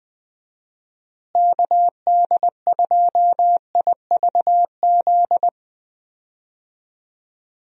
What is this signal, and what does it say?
Morse code, heard as a keyed tone, about 20 wpm, 700 Hz. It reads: KD2IVZ
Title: USA Call Signs - 6 Characters - Repeated 2x - 20wpm